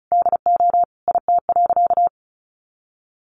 Morse code, heard as a keyed tone, about 34 wpm, 700 Hz. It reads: DO IT.
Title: Sentences from Top 1400 Words 35wpm